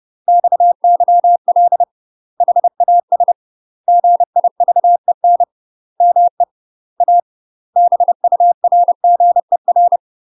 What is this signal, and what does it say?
Morse code, heard as a keyed tone, about 30 wpm, 700 Hz. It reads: XYL HAS GIVEN ME A BURGER